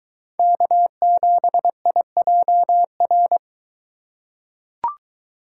Morse code, heard as a keyed tone, about 22 wpm, 700 Hz. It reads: K7IJR